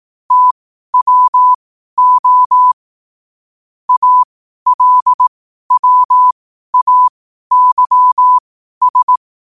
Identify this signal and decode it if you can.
Morse code, heard as a keyed tone, about 18 wpm, spaced as for 12 wpm, 1000 Hz. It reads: TWO ALWAYS
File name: qotdfiles / 12wpm